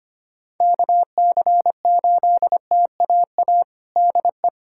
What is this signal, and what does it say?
Morse code, heard as a keyed tone, about 25 wpm, 700 Hz. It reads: KC8TAA DE
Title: USA Callsign DE Callsign - 6 Characters - 25wpm